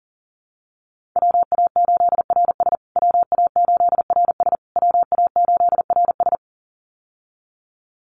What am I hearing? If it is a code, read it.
Morse code, heard as a keyed tone, about 40 wpm, 700 Hz. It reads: WA8RS WA8RS WA8RS
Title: USA Call Signs Repeated 3x 40wpm